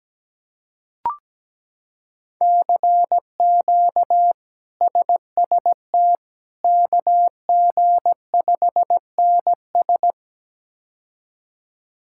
Morse code, heard as a keyed tone, about 17 wpm, 700 Hz. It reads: CQ SST KG5NS
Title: K1USN SST QSO Practice - 17wpm